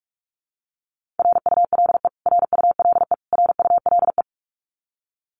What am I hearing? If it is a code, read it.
Morse code, heard as a keyed tone, about 44 wpm, 700 Hz. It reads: RULE RULE RULE
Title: Top 500 Words Repeated 45wpm